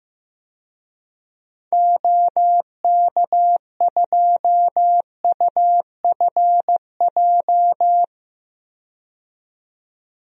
Morse code, heard as a keyed tone, about 16 wpm, 700 Hz. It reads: OK2UFJ
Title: DXCC-100 Call Signs - 6 Characters - 15wpm